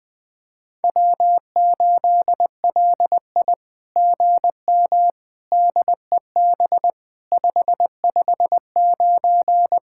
Morse code, heard as a keyed tone, about 19 wpm, 700 Hz. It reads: W8LI GM DEB 559
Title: SOTA QSO Practice 20wpm